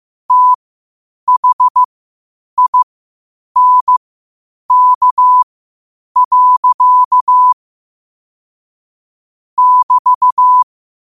Morse code, heard as a keyed tone, about 15 wpm, spaced as for 8 wpm, 1000 Hz. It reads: THINK. <BT>
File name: qotdfiles / 8wpm